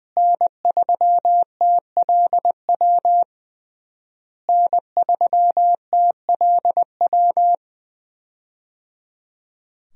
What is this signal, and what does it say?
Morse code, heard as a keyed tone, about 20 wpm, 700 Hz. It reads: N3TLW N3TLW
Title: USA Call Signs - 5 Characters - Repeated 3x - 20wpm